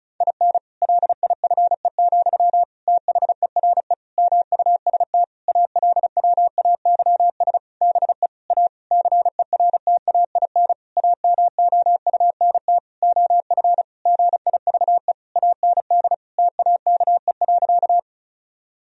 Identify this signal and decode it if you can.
Morse code, heard as a keyed tone, about 36 wpm, 700 Hz. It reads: IN LIFE, THERE MUST ALWAYS BE A CERTAIN AMOUNT OF GIVE AND TAKE.
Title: Sentences from Top 1000 Words - Review Part 4 - 35wpm